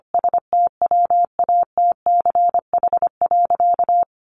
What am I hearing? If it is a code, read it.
Morse code, heard as a keyed tone, about 25 wpm, 700 Hz. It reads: STWATCH.
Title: A Sound Of Thunder 25wpm